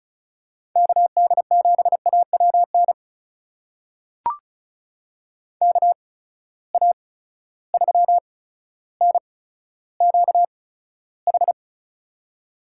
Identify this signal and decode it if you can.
Morse code, heard as a keyed tone, about 34 wpm, spaced as for 15 wpm, 700 Hz. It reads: KD7AWN KA3NQH
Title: USA Call Signs - 6 Characters - 8x Character Spacing - 35wpm